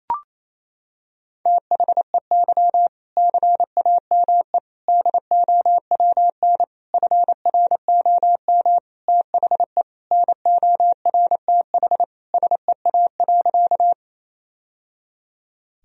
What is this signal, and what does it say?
Morse code, heard as a keyed tone, about 28 wpm, 700 Hz. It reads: THEY CAME DOWN FROM THE NORTH SEA.
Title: Sentences from Top 1000 Words - Review Part 1 - 28wpm